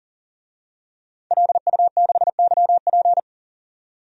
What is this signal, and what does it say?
Morse code, heard as a keyed tone, about 37 wpm, 700 Hz. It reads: LU6YP
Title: DXCC-100 Call Signs - 5 Characters - 40wpm